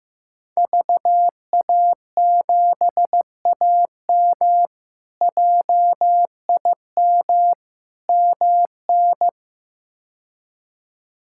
Morse code, heard as a keyed tone, about 15 wpm, 700 Hz. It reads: VA7AM JIM MN
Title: K1USN SST QSO Practice - 15wpm